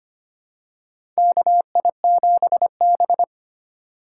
Morse code, heard as a keyed tone, about 25 wpm, 700 Hz. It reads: KI7B
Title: USA Call Signs - 4 Characters - 2x Character Spacing - 25wpm